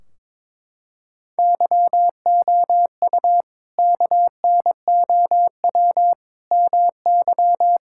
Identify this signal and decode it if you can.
Morse code, heard as a keyed tone, about 21 wpm, 700 Hz. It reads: YOU KNOW MY
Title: Sentences from Top 1600 Words 22wpm